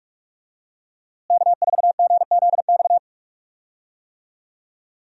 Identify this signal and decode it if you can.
Morse code, heard as a keyed tone, about 46 wpm, 700 Hz. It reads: K4GZX
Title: USA Call Signs - 5 Characters - Repeated 2x - 45wpm